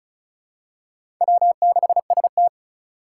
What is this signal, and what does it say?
Morse code, heard as a keyed tone, about 36 wpm, 700 Hz. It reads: W6ST
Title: USA Call Signs - 4 Characters - 3x Character Spacing - 35wpm